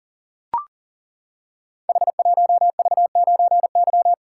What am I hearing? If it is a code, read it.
Morse code, heard as a keyed tone, about 38 wpm, 700 Hz. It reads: H1V9Y
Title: Code Groups - 5 Letter:Numbers - 40wpm